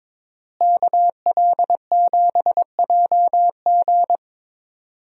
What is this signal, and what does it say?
Morse code, heard as a keyed tone, about 21 wpm, 700 Hz. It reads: KL7JG
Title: DXCC-100 Call Signs - 5 Characters - Repeated 2x - 22wpm